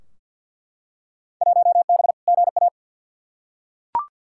Morse code, heard as a keyed tone, about 50 wpm, 700 Hz. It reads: 1B GA
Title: Field Day Exchanges Encore - 50wpm